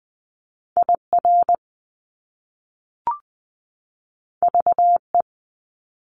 Morse code, heard as a keyed tone, about 20 wpm, 700 Hz. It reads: IR VE
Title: Top Two Letter Combinations in Words 20wpm